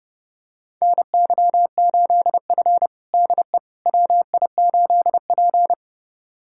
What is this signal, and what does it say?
Morse code, heard as a keyed tone, about 30 wpm, 700 Hz. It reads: NY8F DE WI8P
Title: USA Callsign DE Callsign - 4 Characters - 30wpm